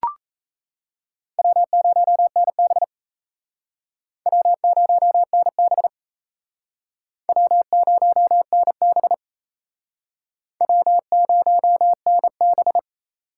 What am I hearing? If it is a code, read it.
Morse code, heard as a keyed tone, about 37 wpm, 700 Hz. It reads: W0NB W0NB W0NB W0NB